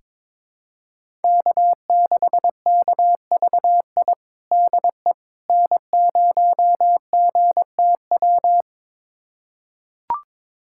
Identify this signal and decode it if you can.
Morse code, heard as a keyed tone, about 22 wpm, 700 Hz. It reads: K6KVI DE N0GTW
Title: USA Callsign DE Callsign - 5 Characters - 22wpm